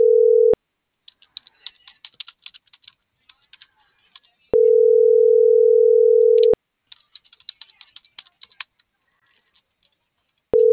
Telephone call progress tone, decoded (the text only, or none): ringback tone